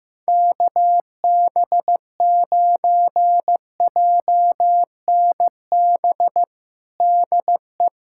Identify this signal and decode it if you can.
Morse code, heard as a keyed tone, about 15 wpm, 700 Hz. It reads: KB9JNB DE
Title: USA Callsign DE Callsign 15wpm